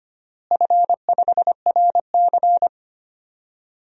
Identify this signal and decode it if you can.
Morse code, heard as a keyed tone, about 25 wpm, 700 Hz. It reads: F5RC
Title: DXCC-100 Call Signs - 4 Characters - Repeated 2x - 25wpm